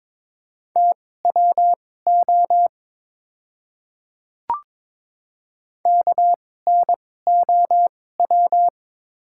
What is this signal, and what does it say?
Morse code, heard as a keyed tone, about 22 wpm, spaced as for 16 wpm, 700 Hz. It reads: TWO KNOW
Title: 2x Character Spacing - Top 200 Words - 22wpm